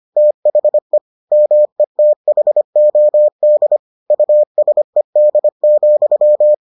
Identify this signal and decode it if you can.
Morse code, heard as a keyed tone, about 25 wpm, 600 Hz. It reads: THE METHOD USED,